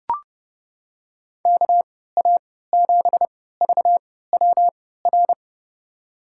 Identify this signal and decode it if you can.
Morse code, heard as a keyed tone, about 29 wpm, spaced as for 17 wpm, 700 Hz. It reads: KA7VWR
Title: USA Call Signs - 3x Character Spacing - 30wpm